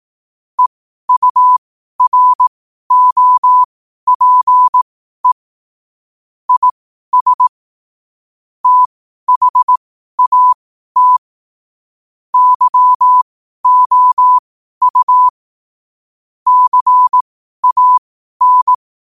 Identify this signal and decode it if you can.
Morse code, heard as a keyed tone, about 18 wpm, spaced as for 12 wpm, 1000 Hz. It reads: EUROPE IS THAT YOU CAN
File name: qotdfiles / 12wpm